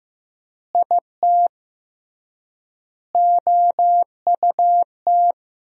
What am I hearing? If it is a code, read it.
Morse code, heard as a keyed tone, about 15 wpm, 700 Hz. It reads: IT OUT
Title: 3x Word-Spacing - Sentences from Top 200 Words - 15wpm